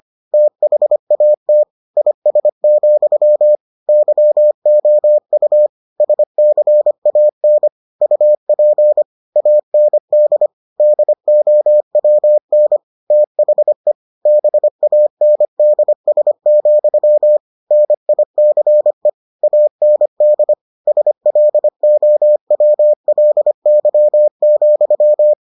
Morse code, heard as a keyed tone, about 25 wpm, 600 Hz. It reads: THAT IS, YOU SCAN UP AND DOWN THE BANDS, NICE AND SLOWLY,